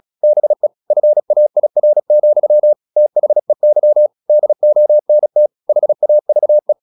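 Morse code, heard as a keyed tone, about 36 wpm, 600 Hz. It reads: BE FAIR, THEY DONT HAVE
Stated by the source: Morse practice at 36 wpm, text of Ken Follett's novel Never, from CW med Gnister.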